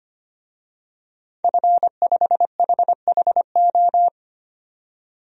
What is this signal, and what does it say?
Morse code, heard as a keyed tone, about 25 wpm, 700 Hz. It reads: F5HHO
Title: DXCC-100 Call Signs - 5 Characters - 25wpm